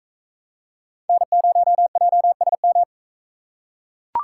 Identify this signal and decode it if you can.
Morse code, heard as a keyed tone, about 41 wpm, 700 Hz. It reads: N0JSM